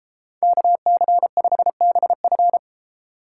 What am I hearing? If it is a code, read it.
Morse code, heard as a keyed tone, about 38 wpm, 700 Hz. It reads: KC5BF